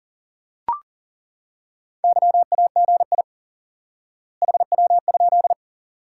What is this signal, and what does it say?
Morse code, heard as a keyed tone, about 40 wpm, 700 Hz. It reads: YAGI HW?
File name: IQE Course 3 - Lesson 89 - YAGI - Two Elements - 40wpm